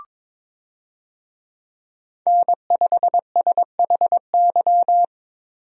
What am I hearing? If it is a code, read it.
Morse code, heard as a keyed tone, about 22 wpm, 700 Hz. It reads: N5SHY
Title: USA Call Signs - 5 Characters - Repeated 2x - 22wpm